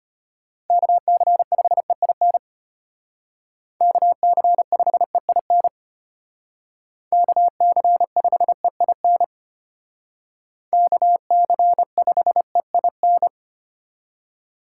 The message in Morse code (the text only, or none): KC5EIN KC5EIN KC5EIN KC5EIN